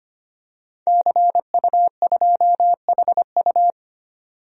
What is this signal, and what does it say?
Morse code, heard as a keyed tone, about 25 wpm, 700 Hz. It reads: CU2HU
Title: DXCC-100 Call Signs - 5 Characters - 25wpm